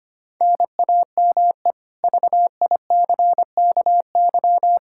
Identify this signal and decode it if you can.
Morse code, heard as a keyed tone, about 25 wpm, 700 Hz. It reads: NAME VICKY
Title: K1USN SST QSO Practice 25wpm